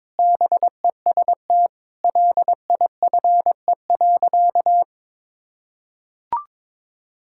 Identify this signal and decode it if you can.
Morse code, heard as a keyed tone, about 22 wpm, 700 Hz. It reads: BEST LIFE.
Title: Sentences from Top 1000 Words - Review Part 6 - 22wpm